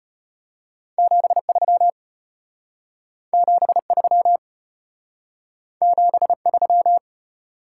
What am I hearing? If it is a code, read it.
Morse code, heard as a keyed tone, about 33 wpm, 700 Hz. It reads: 73 73 73